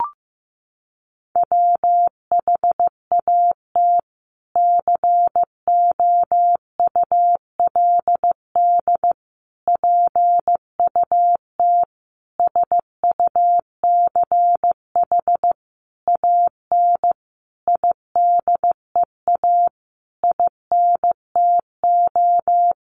Morse code, heard as a keyed tone, about 15 wpm, 700 Hz. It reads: WHAT COULD PUT SUCH AN IDEA INTO
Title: Sentences from Top 600 Words 15wpm